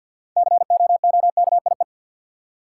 Morse code, heard as a keyed tone, about 52 wpm, 700 Hz. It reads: COOKIE
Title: Six Letter Words - 50wpm